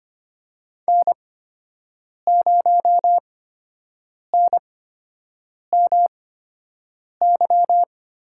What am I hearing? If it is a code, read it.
Morse code, heard as a keyed tone, about 25 wpm, spaced as for 7 wpm, 700 Hz. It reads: N0NMY
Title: USA Call Signs - 5 Characters - 8x Character Spacing - 25wpm